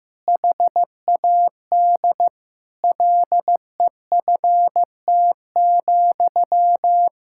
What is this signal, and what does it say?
Morse code, heard as a keyed tone, about 16 wpm, 700 Hz. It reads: HAD LEFT,
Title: Sentences from Top 1000 Words - Review Part 5 - 15wpm